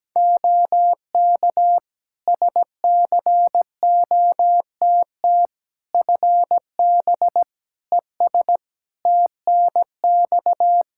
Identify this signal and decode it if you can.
Morse code, heard as a keyed tone, about 17 wpm, 700 Hz. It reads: OK SCOTT FB ES TNX
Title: Practice QSO - Exchange 2 - Info - 17wpm